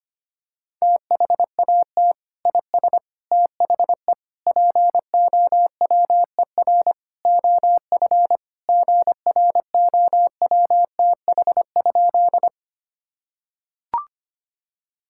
Morse code, heard as a keyed tone, about 25 wpm, 700 Hz. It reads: THAT IS THE POWER OF GROWTH?
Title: Sentences from Top 1000 Words - Review Part 5 - 25wpm